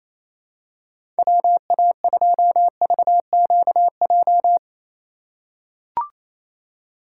Morse code, heard as a keyed tone, about 28 wpm, 700 Hz. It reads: WA2VQJ
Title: USA Call Signs - 4x Character Spacing - 28wpm